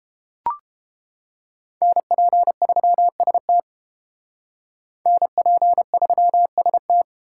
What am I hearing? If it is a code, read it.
Morse code, heard as a keyed tone, about 30 wpm, 700 Hz. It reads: NP3ST NP3ST